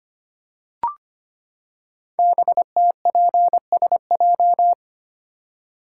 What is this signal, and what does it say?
Morse code, heard as a keyed tone, about 25 wpm, 700 Hz. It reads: BTPSJ
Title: Code Groups - 5 Letters - 25wpm